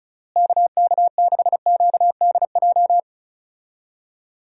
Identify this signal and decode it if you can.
Morse code, heard as a keyed tone, about 36 wpm, 700 Hz. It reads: KK6QDJ
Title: USA Call Signs - 6 Characters - Repeated 3x - 35wpm